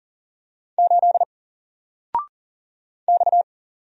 Morse code, heard as a keyed tone, about 41 wpm, 700 Hz. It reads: 8 X
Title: Single Character (Letter:Number:Punctuation:Prosign) 40wpm